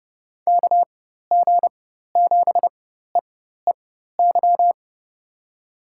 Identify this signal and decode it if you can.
Morse code, heard as a keyed tone, about 32 wpm, spaced as for 14 wpm, 700 Hz. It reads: KG7EEY
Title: USA Call Signs - 6 Characters - 4x Character Spacing - 30wpm